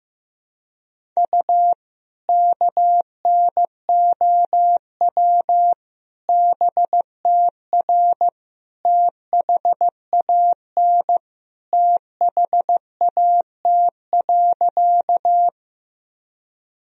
Morse code, heard as a keyed tone, about 15 wpm, 700 Hz. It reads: U KNOW BTR THAN THAT.
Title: Sentences from Top 400 Words and Abbreviations 15wpm